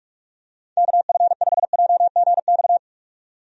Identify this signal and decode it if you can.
Morse code, heard as a keyed tone, about 44 wpm, 700 Hz. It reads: KF5JGX
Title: USA Call Signs - 6 Characters - 45wpm